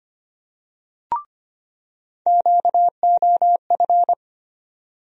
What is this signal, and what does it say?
Morse code, heard as a keyed tone, about 25 wpm, 700 Hz. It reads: QOF
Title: USA Call Sign Postfixes - 3 Letters 25wpm